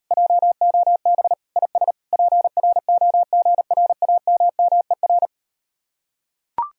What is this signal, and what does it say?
Morse code, heard as a keyed tone, about 39 wpm, 700 Hz. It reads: JOB IS PROGRAMMER